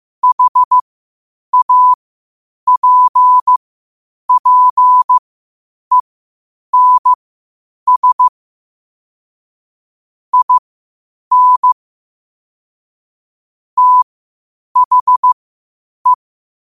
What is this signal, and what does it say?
Morse code, heard as a keyed tone, about 15 wpm, spaced as for 8 wpm, 1000 Hz. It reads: HAPPENS IN THE
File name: qotdfiles / 8wpm